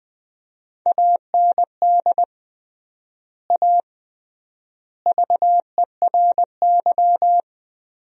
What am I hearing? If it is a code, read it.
Morse code, heard as a keyed tone, about 20 wpm, 700 Hz. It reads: AND A VERY
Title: 3x Word-Spacing - Sentences from Top 200 Words - 20wpm